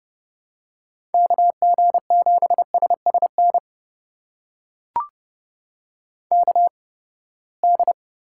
Morse code, heard as a keyed tone, about 32 wpm, spaced as for 21 wpm, 700 Hz. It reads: KG7SSN KD
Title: USA Call Signs - 6 Characters - 8x Character Spacing - 30wpm